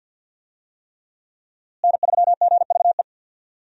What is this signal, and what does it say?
Morse code, heard as a keyed tone, about 52 wpm, 700 Hz. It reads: N3GVE
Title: USA Call Signs - 5 Characters - Repeated 3x - 50wpm